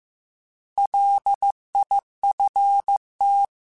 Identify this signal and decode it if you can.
Morse code, heard as a keyed tone, about 15 wpm, 800 Hz. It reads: LIFT